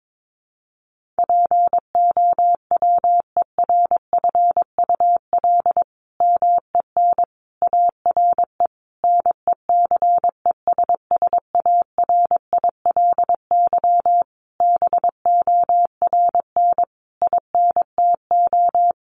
Morse code, heard as a keyed tone, about 22 wpm, 700 Hz. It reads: POWERFUL MEN ARE NECESSARILY BORN INTO
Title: Sentences from Top 1400 Words 22wpm